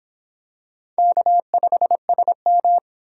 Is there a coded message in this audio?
Yes: Morse code from 1.0 to 2.8 s, heard as a keyed tone, about 22 wpm, 700 Hz. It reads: K5SM